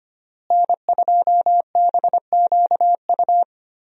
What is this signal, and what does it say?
Morse code, heard as a keyed tone, about 25 wpm, 700 Hz. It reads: N2BQU
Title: USA Call Signs - 5 Characters - Repeated 3x - 25wpm